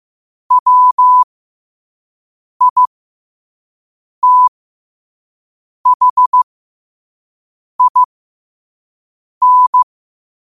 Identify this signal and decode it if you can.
Morse code, heard as a keyed tone, about 15 wpm, spaced as for 5 wpm, 1000 Hz. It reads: WITHIN
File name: qotdfiles / 5wpm